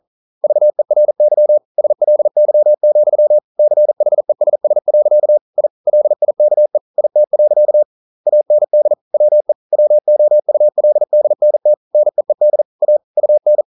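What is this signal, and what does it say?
Morse code, heard as a keyed tone, about 41 wpm, 600 Hz. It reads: VERY SLY, CHESS. I LIKE IT. AND WE WOULDNT NEED A UN
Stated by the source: Morse code practice at 41 wpm, from CW med Gnister.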